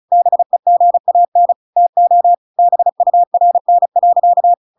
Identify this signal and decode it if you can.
Morse code, heard as a keyed tone, about 34 wpm, 700 Hz. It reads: BEGAN TO BURN.